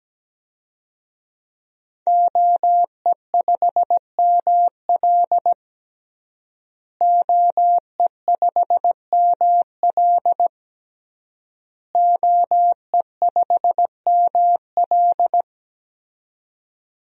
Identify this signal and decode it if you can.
Morse code, heard as a keyed tone, about 17 wpm, 700 Hz. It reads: OE5ML OE5ML OE5ML
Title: DXCC-100 Call Signs - 5 Characters - Repeated 3x - 17wpm